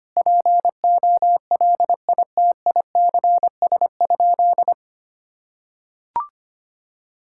Morse code, heard as a keyed tone, about 25 wpm, 700 Hz. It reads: POLITICS?
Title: Sentences from Top 1000 Words - Review Part 6 - 25wpm